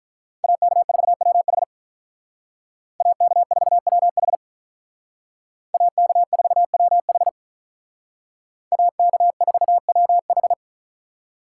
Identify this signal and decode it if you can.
Morse code, heard as a keyed tone, about 45 wpm, 700 Hz. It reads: AK4WH AK4WH AK4WH AK4WH